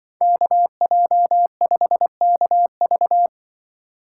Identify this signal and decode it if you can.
Morse code, heard as a keyed tone, about 25 wpm, 700 Hz. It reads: KJ5KV
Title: Speed Racing - USA 2x2 Call Signs 20wpm